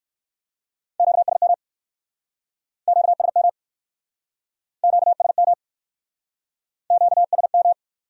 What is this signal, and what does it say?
Morse code, heard as a keyed tone, about 60 wpm, 700 Hz. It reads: QSM QSM QSM QSM